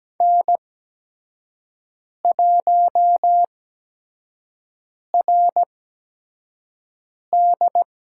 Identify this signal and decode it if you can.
Morse code, heard as a keyed tone, about 17 wpm, spaced as for 5 wpm, 700 Hz. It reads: N1RD
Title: USA Call Signs - 8x Character Spacing - 17wpm